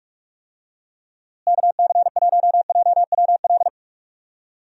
Morse code, heard as a keyed tone, about 41 wpm, 700 Hz. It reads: KC1JWL
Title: USA Call Signs - 6 Characters - Repeated 2x - 45wpm